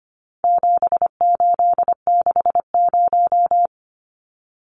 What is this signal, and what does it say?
Morse code, heard as a keyed tone, about 25 wpm, 700 Hz. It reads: 7860
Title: Numbers - 4-Digit 25wpm